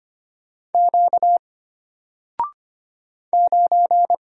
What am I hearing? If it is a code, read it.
Morse code, heard as a keyed tone, about 25 wpm, 700 Hz. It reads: Q 9
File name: Single-Letter-Number-25wpm